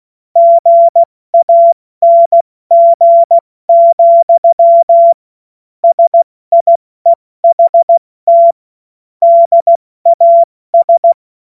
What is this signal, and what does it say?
Morse code, heard as a keyed tone, about 16 wpm, spaced as for 14 wpm, 680 Hz. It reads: GANG, SIEHT DAS
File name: Morsekurs / Witze / Witz 14. Juni 2022 70 F80